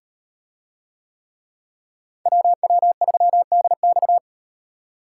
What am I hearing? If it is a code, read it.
Morse code, heard as a keyed tone, about 38 wpm, 700 Hz. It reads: WW3DX